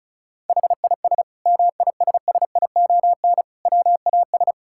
Morse code, heard as a keyed tone, about 36 wpm, 700 Hz. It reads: HIS MISSION WAS
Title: Sentences from Top 1600 Words 35wpm